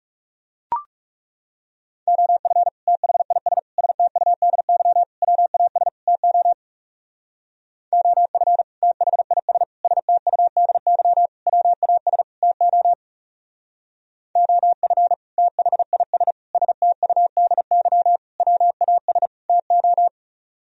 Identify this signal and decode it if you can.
Morse code, heard as a keyed tone, about 36 wpm, 700 Hz. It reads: OF THIS STUDY WAS TO OF THIS STUDY WAS TO OF THIS STUDY WAS TO
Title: Speed-Racing - Sets of 5 Words 30wpm